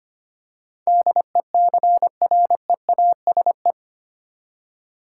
Morse code, heard as a keyed tone, about 25 wpm, 700 Hz. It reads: DECREASE
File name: Eight Letter Words - 25wpm